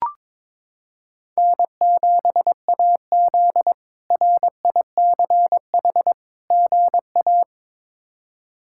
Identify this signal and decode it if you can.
Morse code, heard as a keyed tone, about 22 wpm, 700 Hz. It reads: N7AZ RICH GA
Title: K1USN SST QSO Practice - 22wpm